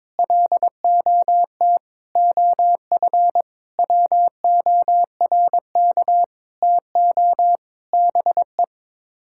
Morse code, heard as a keyed tone, about 22 wpm, 700 Hz. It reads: LOT OF WORK TO BE
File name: Sets of 9 Words - 22wpm